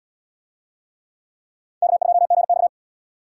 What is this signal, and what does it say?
Morse code, heard as a keyed tone, about 63 wpm, 700 Hz. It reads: K2GX